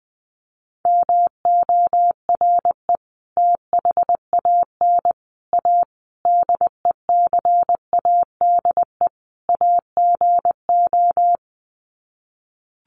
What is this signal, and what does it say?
Morse code, heard as a keyed tone, about 20 wpm, 700 Hz. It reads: MORE THAN A DECADE AGO
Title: Sets of 5 Words Encore2 20wpm